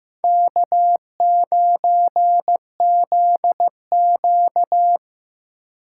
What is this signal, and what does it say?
Morse code, heard as a keyed tone, about 16 wpm, 700 Hz. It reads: K9ZQ
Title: USA Call Signs - 4 Characters - 4x Character Spacing - 15wpm